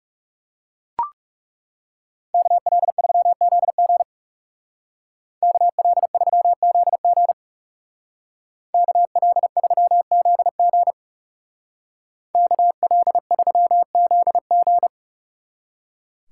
Morse code, heard as a keyed tone, about 39 wpm, 700 Hz. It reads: KL3ZG KL3ZG KL3ZG KL3ZG